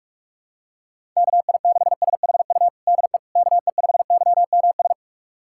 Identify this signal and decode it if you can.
Morse code, heard as a keyed tone, about 41 wpm, 690 Hz. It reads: KI6SHU DE KE5YMS
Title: USA Callsign DE Callsign - 6 Characters - 45wpm